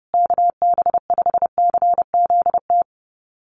Morse code, heard as a keyed tone, about 30 wpm, 700 Hz. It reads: KB5CZT
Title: USA Call Signs Repeated 2x 30wpm